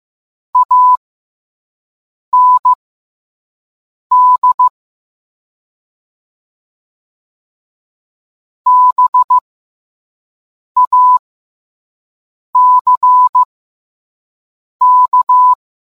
Morse code, heard as a keyed tone, about 15 wpm, spaced as for 5 wpm, 1000 Hz. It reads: AND BACK